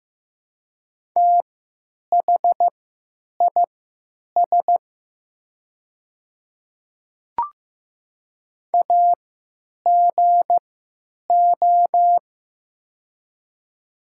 Morse code, heard as a keyed tone, about 16 wpm, spaced as for 9 wpm, 700 Hz. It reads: THIS AGO
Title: 3x Character Spacing - Top 500 Words - 15wpm